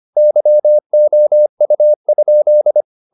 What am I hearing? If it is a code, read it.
Morse code, heard as a keyed tone, about 25 wpm, 600 Hz. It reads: YOU?
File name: the old man and the sea part 1 25wpm